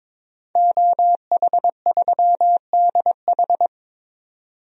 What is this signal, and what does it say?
Morse code, heard as a keyed tone, about 21 wpm, 700 Hz. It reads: OH3DH
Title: DXCC-100 Call Signs - 5 Characters - 22wpm